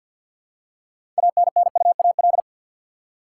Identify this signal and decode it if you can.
Morse code, heard as a keyed tone, about 47 wpm, 700 Hz. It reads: ANNUAL
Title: Six Letter Words - 50wpm